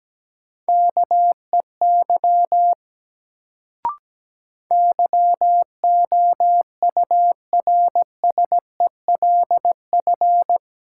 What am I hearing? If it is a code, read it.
Morse code, heard as a keyed tone, about 17 wpm, 700 Hz. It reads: KEY YOURSELF
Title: Sentences from Top 600 Words - New Word Drill - 17wpm